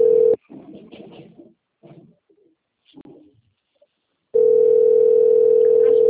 Ringback tone (North America), 440 + 480 Hz.